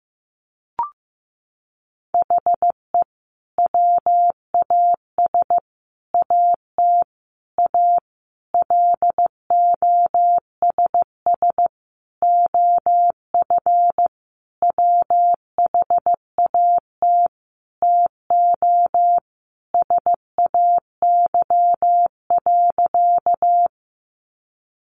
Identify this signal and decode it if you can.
Morse code, heard as a keyed tone, about 14 wpm, 700 Hz. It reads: HE WAS AT A LOSS OF WHAT TO SAY.
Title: Sentences from Top 1400 Words 15wpm